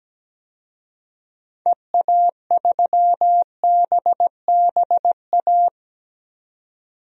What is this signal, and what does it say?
Morse code, heard as a keyed tone, about 17 wpm, 700 Hz. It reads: EA3BBA
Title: DXCC-100 Call Signs - 6 Characters - 17wpm